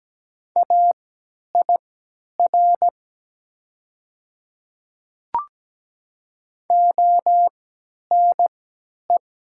Morse code, heard as a keyed tone, about 17 wpm, spaced as for 10 wpm, 700 Hz. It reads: AIR ONE
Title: 3x Character Spacing - Top 200 Words - 17wpm